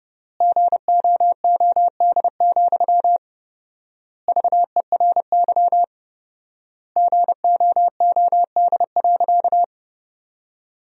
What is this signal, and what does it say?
Morse code, heard as a keyed tone, about 30 wpm, 700 Hz. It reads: GOOD, VERY GOOD.
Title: 4x Word-Spacing - Sentences from Top 200 Words - 30wpm